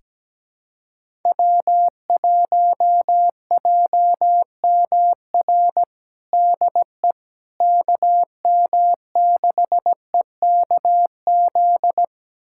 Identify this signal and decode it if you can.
Morse code, heard as a keyed tone, about 17 wpm, 700 Hz. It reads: W1JMR DE KM6EKZ
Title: USA Callsign DE Callsign 17wpm